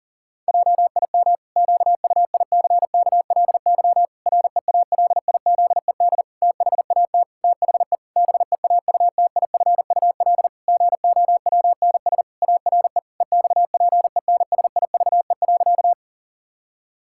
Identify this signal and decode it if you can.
Morse code, heard as a keyed tone, about 40 wpm, 700 Hz. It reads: JIM QUICKLY REALIZED THAT THE BEAUTIFUL GOWNS ARE EXPENSIVE.
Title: Pangram Challenge v2 - 40wpm